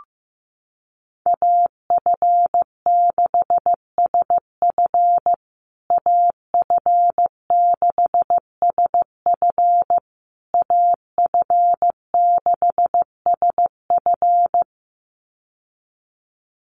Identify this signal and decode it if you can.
Morse code, heard as a keyed tone, about 15 wpm, 700 Hz. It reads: AF6SF AF6SF AF6SF
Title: USA Call Signs Repeated 3x 15wpm